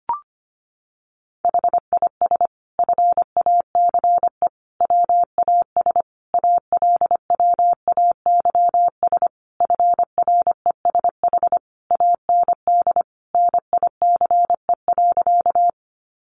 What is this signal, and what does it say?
Morse code, heard as a keyed tone, about 25 wpm, 700 Hz. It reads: HIS FACE WAS ALWAYS FRESH AND NICE.
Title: Sentences from Top 1200 Words 25wpm